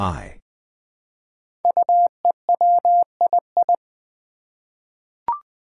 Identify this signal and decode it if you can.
Morse code, heard as a keyed tone, about 20 wpm, 700 Hz. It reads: UEWII
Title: Five Challenging Letter-Number Combinations 20wpm